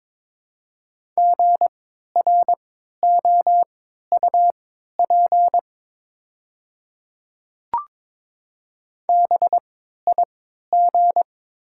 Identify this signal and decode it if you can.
Morse code, heard as a keyed tone, about 22 wpm, spaced as for 12 wpm, 700 Hz. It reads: GROUP BIG
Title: 3x Character Spacing - Top 300 Words - 22wpm